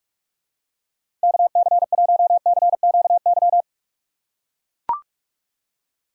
Morse code, heard as a keyed tone, about 42 wpm, 700 Hz. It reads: KC1CQY
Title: USA Call Signs - 6 Characters - 2x Character Spacing - 45wpm